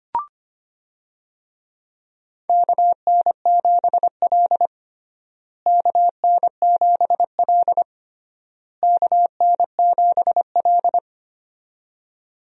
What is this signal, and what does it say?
Morse code, heard as a keyed tone, about 25 wpm, 700 Hz. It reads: KN7L KN7L KN7L
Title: USA Call Signs - 4 Characters - Repeated 3x - 25wpm